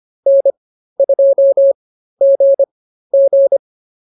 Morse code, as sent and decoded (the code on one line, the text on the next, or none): -. ..--- --. --.
N2GG